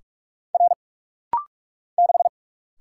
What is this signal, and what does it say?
Morse code, heard as a keyed tone, about 46 wpm, 700 Hz. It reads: R 6
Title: Single Letter-Number - Rapid-Fire 45wpm